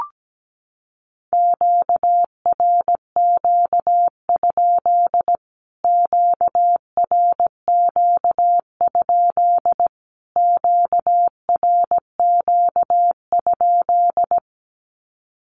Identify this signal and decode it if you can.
Morse code, heard as a keyed tone, about 17 wpm, 700 Hz. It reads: QRQ? QRQ? QRQ?
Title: Q-Codes Repeated 3x 17wpm